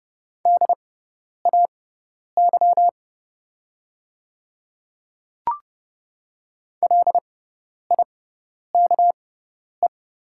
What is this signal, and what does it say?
Morse code, heard as a keyed tone, about 30 wpm, spaced as for 10 wpm, 690 Hz. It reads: DAY LIKE
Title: 6x Character Spacing - Top 300 Words - 30wpm